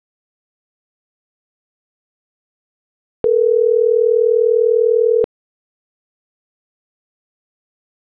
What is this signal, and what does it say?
Ringback tone (North America), 440 + 480 Hz.